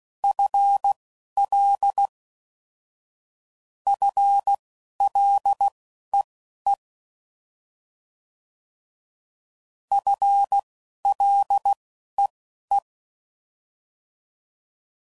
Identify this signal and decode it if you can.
Morse code, heard as a keyed tone, about 16 wpm, spaced as for 12 wpm, 800 Hz. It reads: FL FLEE FLEE
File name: Mixed / Short Runs / FL Prefixes